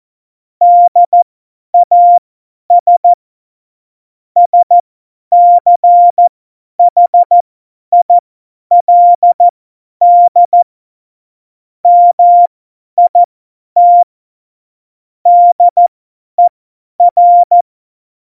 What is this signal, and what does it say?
Morse code, heard as a keyed tone, about 14 wpm, spaced as for 10 wpm, 700 Hz. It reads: DAS SCHILD MIT DER
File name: Morsekurs / Witze / Witz 02 Juni 2020, 50 F70